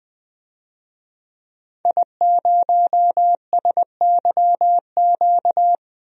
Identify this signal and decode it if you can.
Morse code, heard as a keyed tone, about 20 wpm, 700 Hz. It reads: I0SYQ
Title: DXCC-100 Call Signs - 5 Characters - Repeated 3x - 20wpm